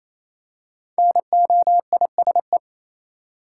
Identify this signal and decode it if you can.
Morse code, heard as a keyed tone, about 28 wpm, 700 Hz. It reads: NOISE
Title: Five Letter Words - 28wpm